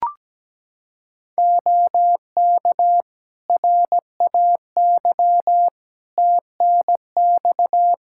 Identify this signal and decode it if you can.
Morse code, heard as a keyed tone, about 17 wpm, 700 Hz. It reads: OK RAY TNX
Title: Practice QSO - Exchange 4 - Ending - 17wpm